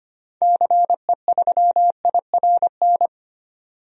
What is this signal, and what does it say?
Morse code, heard as a keyed tone, about 25 wpm, 700 Hz. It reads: CE3IRN
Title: DXCC-100 Call Signs - 6 Characters - 25wpm